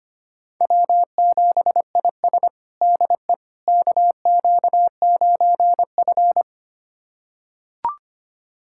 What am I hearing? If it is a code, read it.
Morse code, heard as a keyed tone, about 25 wpm, 700 Hz. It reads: W7IS DE KQ9F
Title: USA Callsign DE Callsign - 4 Characters - 25wpm